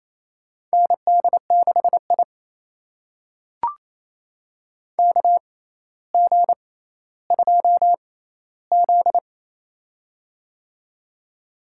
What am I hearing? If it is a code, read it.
Morse code, heard as a keyed tone, about 28 wpm, spaced as for 14 wpm, 700 Hz. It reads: ND6I KG2Z
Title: USA Call Signs - 4 Characters - 6x Character Spacing - 28wpm